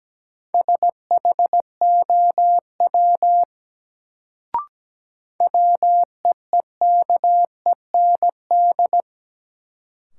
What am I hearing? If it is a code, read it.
Morse code, heard as a keyed tone, about 17 wpm, 700 Hz. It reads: SHOW WEEKEND
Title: Sets of 5 Words - New Word Drill - 17wpm